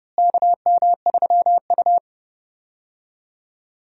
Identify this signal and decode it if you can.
Morse code, heard as a keyed tone, about 32 wpm, 700 Hz. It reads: KM3U
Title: USA Call Signs - 4 Characters - Repeated 2x - 30wpm